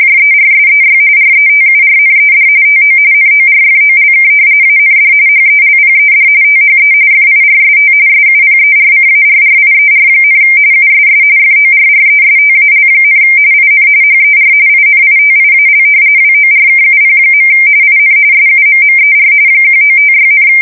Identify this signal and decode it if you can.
RTTY, about 45.45 baud. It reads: JERSEY RETURNED TO NORFOLK AND HELPED THE FIRST SUPERCARRIER, USS FORRESTAL, IN HER INITIAL WORKUP. IN SUMMER 1956,
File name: fsk170 / battleshipnj